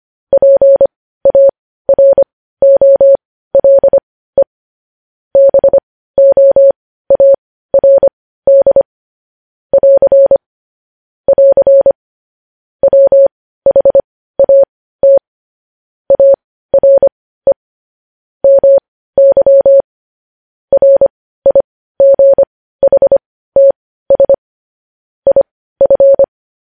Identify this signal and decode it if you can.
Morse code, heard as a keyed tone, about 25 wpm, spaced as for 15 wpm, 550 Hz. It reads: PAROLE BOARD <AR> <AR> WHAT ARE MY RIGHTS IF
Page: news headlines in morse code at 15 wpm